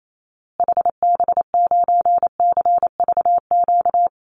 Morse code, heard as a keyed tone, about 28 wpm, 700 Hz. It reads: HB9CVQ
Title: Call Signs Repeated 2x - 28wpm